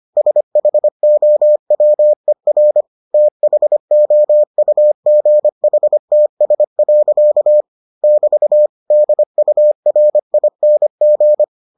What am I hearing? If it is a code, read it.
Morse code, heard as a keyed tone, about 25 wpm, 600 Hz. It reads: SHOWER THOUGHTS. <BT> DURING